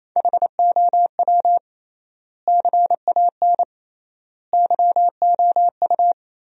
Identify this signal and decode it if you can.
Morse code, heard as a keyed tone, about 28 wpm, 700 Hz. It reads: HOW CAN YOU
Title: 3x Word-Spacing - Sentences from Top 200 Words - 28wpm